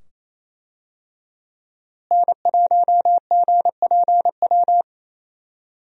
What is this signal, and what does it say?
Morse code, heard as a keyed tone, about 28 wpm, 700 Hz. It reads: N1GPW
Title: USA Call Signs - 5 Characters - Repeated 2x - 28wpm